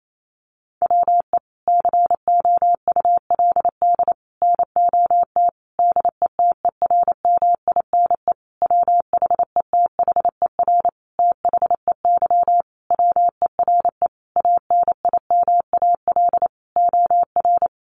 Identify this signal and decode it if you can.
Morse code, heard as a keyed tone, about 28 wpm, 700 Hz. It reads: WE COULD NOT DETERMINE WHETHER THEY WERE ANIMAL OR
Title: Sentences from Top 1300 Words 28wpm